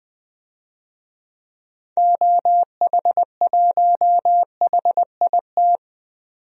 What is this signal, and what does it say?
Morse code, heard as a keyed tone, about 20 wpm, 700 Hz. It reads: OH1HIT
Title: DXCC-100 Call Signs - 6 Characters - 20wpm